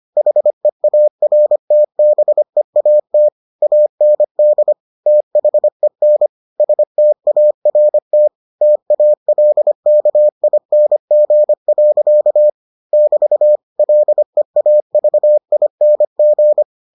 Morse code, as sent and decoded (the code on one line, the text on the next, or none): .... .- .-. - -... .- - / .- -. -.. / - .... -. / ... - .- .-. - / - .- .-.. -.- .. -. --. .-.-.- / -...- / .-.. .- ...- .. -. --.
HEARTBEAT AND THEN START TALKING. <BT> LEAVING